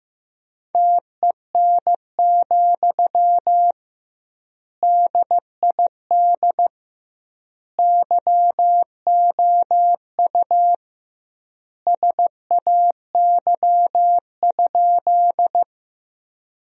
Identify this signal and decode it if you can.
Morse code, heard as a keyed tone, about 15 wpm, 700 Hz. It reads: TEN, DID YOU SAY?
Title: 2x Word-Spacing - Sentences from Top 400 Words - 15wpm